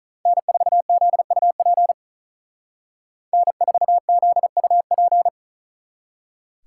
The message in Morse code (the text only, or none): N4ZUP N4ZUP